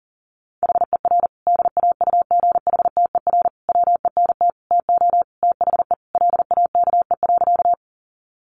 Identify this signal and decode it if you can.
Morse code, heard as a keyed tone, about 40 wpm, 700 Hz. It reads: HER DAUGHTER WENT TO THE LAKE.
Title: Sentences from Top 1500 Words 40wpm